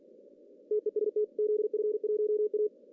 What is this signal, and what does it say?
Morse code, heard as a keyed tone, about 45 wpm, 420 Hz. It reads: TEST ZF1A
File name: CW / cqww-cw-k5tr